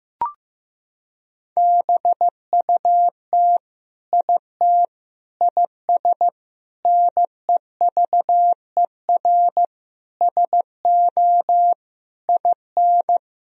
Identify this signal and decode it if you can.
Morse code, heard as a keyed tone, about 15 wpm, 700 Hz. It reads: BUT IT IS NEVER SO IN
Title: Sentences from Top 1000 Words - Review Part 2 - 15wpm